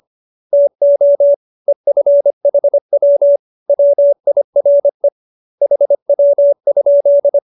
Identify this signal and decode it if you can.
Morse code, heard as a keyed tone, about 25 wpm, 570 Hz. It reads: TO EFHW WIRE HW?
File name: qso207b 25